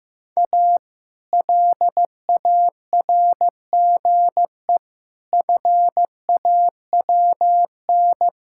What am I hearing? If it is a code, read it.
Morse code, heard as a keyed tone, about 15 wpm, 700 Hz. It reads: A LARGE FAWN
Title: Pangram Challenge v2 - 15wpm